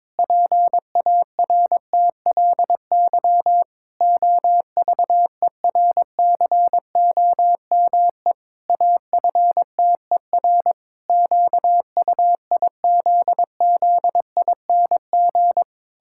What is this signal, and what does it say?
Morse code, heard as a keyed tone, about 22 wpm, 700 Hz. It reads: PARTLY OVERCOME AFTER QUIZZING
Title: Pangram Challenge v2 - 22wpm